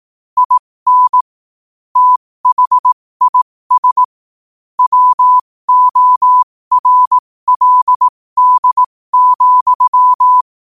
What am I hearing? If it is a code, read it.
Morse code, heard as a keyed tone, about 18 wpm, spaced as for 15 wpm, 1000 Hz. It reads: IN THIS WORLD,
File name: qotdfiles / 15wpm